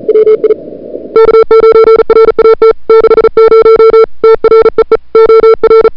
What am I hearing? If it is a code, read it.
Morse code, heard as a keyed tone, about 38 wpm, 440 Hz. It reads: WI K9RAT 60 TREE OR
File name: n6tr / Fab2007CWSprintHour1